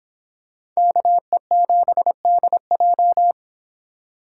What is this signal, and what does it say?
Morse code, heard as a keyed tone, about 26 wpm, 700 Hz. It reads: KE7DJ